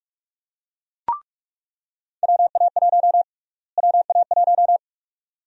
Morse code, heard as a keyed tone, about 43 wpm, 700 Hz. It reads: WA1 WA1
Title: USA Call Sign Prefixes - Repeated 2x - 45wpm